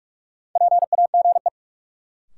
Morse code, heard as a keyed tone, about 46 wpm, 700 Hz. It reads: PAGE
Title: Sentences from Top 1000 Words - Review Part 1 - New Word Drill - 45wpm